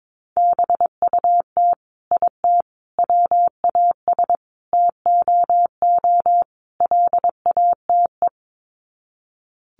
Morse code, heard as a keyed tone, about 22 wpm, 700 Hz. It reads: BUT IT WAS TOO LATE
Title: Sets of 5 Words Encore2 22wpm